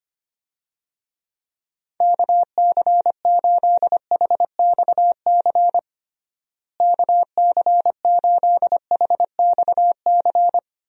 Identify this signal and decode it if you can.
Morse code, heard as a keyed tone, about 25 wpm, 700 Hz. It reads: KC8HXC KC8HXC
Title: USA Call Signs - 6 Characters - Repeated 2x - 25wpm